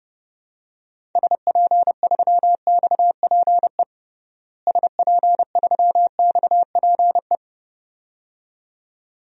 Morse code, heard as a keyed tone, about 32 wpm, 700 Hz. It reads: SP3XPE SP3XPE
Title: DXCC-100 Call Signs - 6 Characters - Repeated 2x - 30wpm